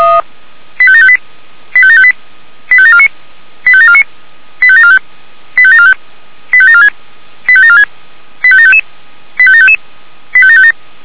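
Five-tone selective call (ZVEI), heard as quick runs of five tones, tones of about 70 ms. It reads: 85758 85758 85749 85749 85745 85745 85746 85746 85755 85755 85757
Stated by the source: Five-tone ZVEI selective call, received on an SDR.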